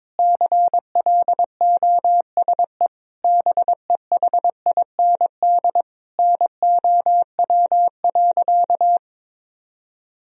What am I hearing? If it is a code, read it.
Morse code, heard as a keyed tone, about 22 wpm, 700 Hz. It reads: CLOSE BEHIND NOW.
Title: Sentences from Top 1000 Words - Review Part 1 - 22wpm